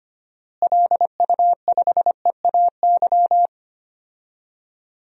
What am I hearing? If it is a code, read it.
Morse code, heard as a keyed tone, about 25 wpm, 700 Hz. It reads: LU5EAY
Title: DXCC-100 Call Signs - 6 Characters - Repeated 3x - 25wpm